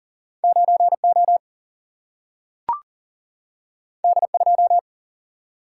Morse code, heard as a keyed tone, about 40 wpm, 700 Hz. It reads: 9O D2
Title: Single-Letter Single-Number 40wpm